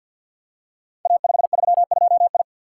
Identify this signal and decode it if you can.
Morse code, heard as a keyed tone, about 51 wpm, 700 Hz. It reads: A532I
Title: Code Groups - 5 Letter:Numbers - 50wpm